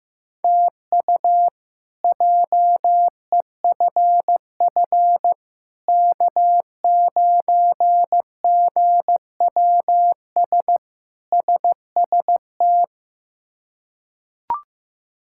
Morse code, heard as a keyed tone, about 15 wpm, 700 Hz. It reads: TU JEFF K9GWS SST
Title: K1USN SST QSO Practice - 15wpm